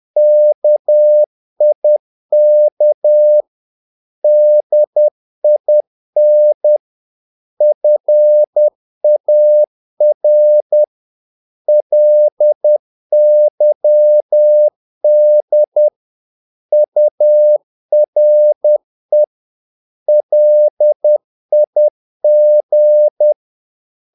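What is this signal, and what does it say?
Morse code, heard as a keyed tone, about 10 wpm, 600 Hz. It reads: KIK DIN FAR LYD URE LIG
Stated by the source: Morse practice of short Danish words at 10 wpm, from CW med Gnister.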